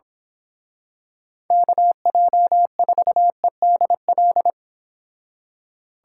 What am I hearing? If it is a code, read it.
Morse code, heard as a keyed tone, about 25 wpm, 700 Hz. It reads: KJ4EDL
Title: Speed Racing - USA 2x3 Call Signs 22wpm